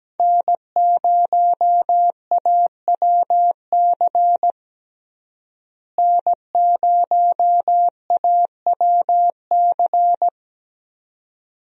Morse code, heard as a keyed tone, about 17 wpm, 700 Hz. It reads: N0AWC N0AWC
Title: USA Call Signs - 5 Characters - Repeated 2x - 17wpm